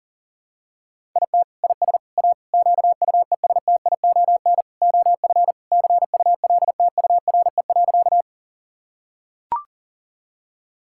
Morse code, heard as a keyed tone, about 37 wpm, 700 Hz. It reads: IT IS A QUESTION OF CULTURE.
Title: Sentences from Top 1000 Words - Review Part 6 - 40wpm